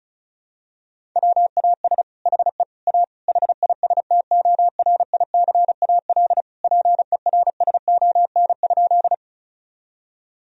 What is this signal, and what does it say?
Morse code, heard as a keyed tone, about 36 wpm, 700 Hz. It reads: WAS HE A HISTORICAL PERSON?
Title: Sentences from Top 1000 Words - Review Part 4 - 35wpm